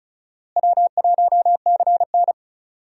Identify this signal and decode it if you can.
Morse code, heard as a keyed tone, about 36 wpm, 700 Hz. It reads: W1CN
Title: USA Call Signs - 4 Characters - Repeated 3x - 35wpm